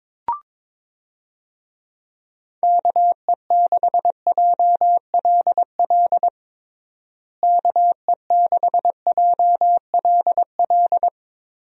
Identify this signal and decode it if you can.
Morse code, heard as a keyed tone, about 22 wpm, 700 Hz. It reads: KE6JLL KE6JLL
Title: USA Call Signs - 6 Characters - Repeated 3x - 22wpm